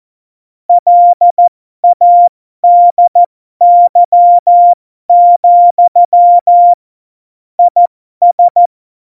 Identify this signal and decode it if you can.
Morse code, heard as a keyed tone, about 14 wpm, spaced as for 12 wpm, 700 Hz. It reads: LADY, IS
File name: Morsekurs / Witze